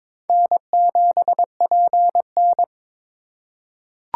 Morse code, heard as a keyed tone, about 22 wpm, 700 Hz. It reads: N7PN
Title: USA Call Signs - 4 Characters - 4x Character Spacing - 22wpm